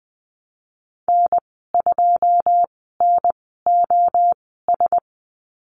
Morse code, heard as a keyed tone, about 20 wpm, spaced as for 15 wpm, 700 Hz. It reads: N2NOS
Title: 088 Lesson 15g - Number 5 - Call Signs 20wpm